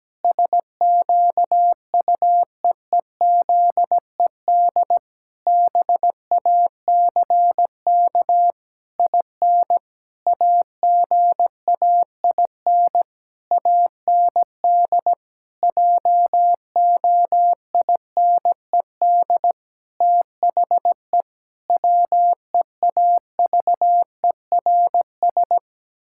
Morse code, heard as a keyed tone, about 17 wpm, 700 Hz. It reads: SQUEEZED BACK IN AGAIN AND JOINED THE WEAVERS
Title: Pangram Challenge v2 - 17wpm